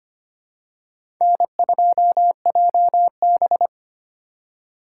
Morse code, heard as a keyed tone, about 25 wpm, 700 Hz. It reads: N2JB
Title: USA Call Signs - 4 Characters - 4x Character Spacing - 25wpm